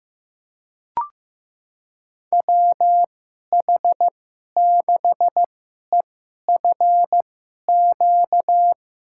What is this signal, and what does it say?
Morse code, heard as a keyed tone, about 16 wpm, spaced as for 11 wpm, 700 Hz. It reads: WH6EFQ
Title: USA Call Signs - 6 Characters - 2x Character Spacing - 15wpm